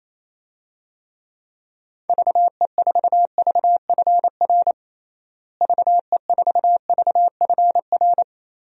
Morse code, heard as a keyed tone, about 28 wpm, 700 Hz. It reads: VE4VFR VE4VFR
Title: DXCC-100 Call Signs - 6 Characters - Repeated 2x - 28wpm